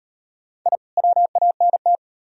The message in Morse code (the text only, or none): I WANT